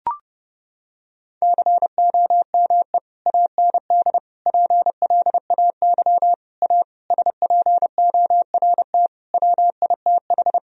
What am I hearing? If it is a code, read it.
Morse code, heard as a keyed tone, about 32 wpm, 700 Hz. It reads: COME AND PLAY A SPORT WITH
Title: Sentences from Top 1000 Words - Review Part 4 - 30wpm